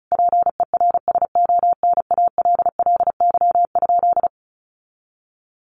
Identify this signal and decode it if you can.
Morse code, heard as a keyed tone, about 34 wpm, 700 Hz. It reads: PERSONALLY?
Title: Sentences from Top 1400 Words 35wpm